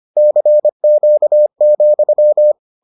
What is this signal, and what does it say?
Morse code, heard as a keyed tone, about 25 wpm, 600 Hz. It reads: CQ,